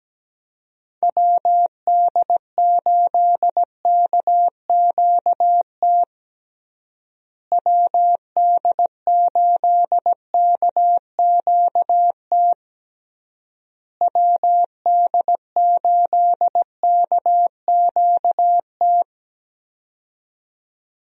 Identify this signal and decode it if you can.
Morse code, heard as a keyed tone, about 17 wpm, 700 Hz. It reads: WD8KQT WD8KQT WD8KQT
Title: USA Call Signs - 6 Characters - Repeated 3x - 17wpm